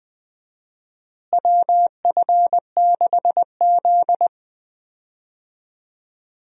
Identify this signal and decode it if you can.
Morse code, heard as a keyed tone, about 20 wpm, 700 Hz. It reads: WF6Z
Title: USA Call Signs - 4 Characters - Repeated 3x - 20wpm